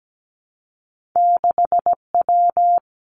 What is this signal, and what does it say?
Morse code, heard as a keyed tone, about 17 wpm, 700 Hz. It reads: 6W
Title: Two Challenging Letter-Number Combinations 17wpm